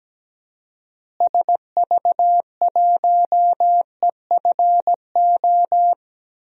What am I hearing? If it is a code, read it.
Morse code, heard as a keyed tone, about 17 wpm, 700 Hz. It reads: SV1EFO
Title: DXCC-100 Call Signs - 6 Characters - Repeated 3x - 17wpm